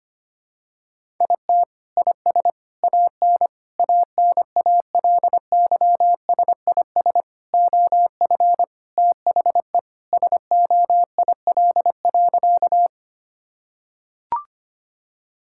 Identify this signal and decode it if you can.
Morse code, heard as a keyed tone, about 25 wpm, 700 Hz. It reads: IT IS AN ANALYSIS OF THE SOIL.
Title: Sentences from Top 1000 Words - Review Part 5 - 25wpm